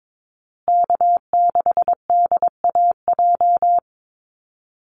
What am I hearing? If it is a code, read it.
Morse code, heard as a keyed tone, about 22 wpm, 700 Hz. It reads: K6DAJ
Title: Call Signs Repeated 2x - 22wpm